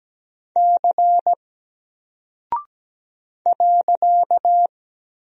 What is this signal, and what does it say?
Morse code, heard as a keyed tone, about 17 wpm, 700 Hz. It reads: C .
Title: Single Character (Letter:Number:Punctuation:Prosign) 17wpm